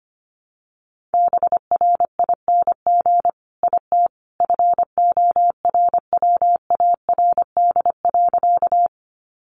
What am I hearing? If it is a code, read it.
Morse code, heard as a keyed tone, about 25 wpm, 700 Hz. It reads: BRING IT FORWARD.
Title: Sentences from Top 1300 Words 25wpm